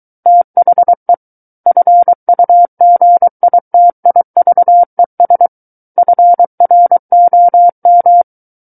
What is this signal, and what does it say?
Morse code, heard as a keyed tone, about 23 wpm, 700 Hz. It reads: THE FUGITIVES FROM
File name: Book One / 14-Chapter 14 - 23 WPM